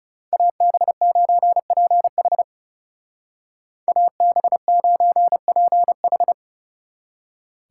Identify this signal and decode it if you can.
Morse code, heard as a keyed tone, about 39 wpm, 700 Hz. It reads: AB9PH AB9PH